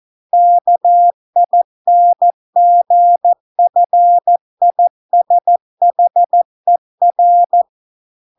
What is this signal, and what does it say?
Morse code, heard as a keyed tone, about 14 wpm, 700 Hz. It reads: KINGFISHER